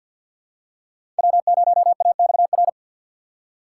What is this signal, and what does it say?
Morse code, heard as a keyed tone, about 52 wpm, 700 Hz. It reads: W0AXR
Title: USA Call Signs - 5 Characters - 6x Character Spacing - 50wpm